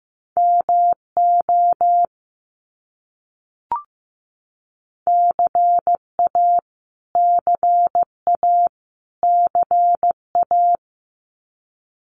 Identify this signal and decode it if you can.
Morse code, heard as a keyed tone, about 15 wpm, 700 Hz. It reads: MO CA CA CA
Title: US States Repeated 3x 15wpm